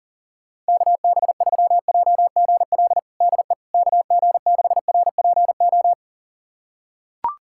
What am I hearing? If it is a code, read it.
Morse code, heard as a keyed tone, about 40 wpm, 700 Hz. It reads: KB3JGL DE KG6RPO
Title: USA Callsign DE Callsign 40wpm